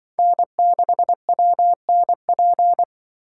Morse code, heard as a keyed tone, about 24 wpm, 700 Hz. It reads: N6WNP
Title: Speed Racing - USA 1x3 Call Signs 20wpm